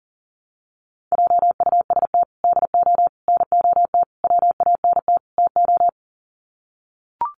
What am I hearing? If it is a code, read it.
Morse code, heard as a keyed tone, about 40 wpm, 700 Hz. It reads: JUST DO NOT WANT TO
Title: Sets of 5 Words Encore 40wpm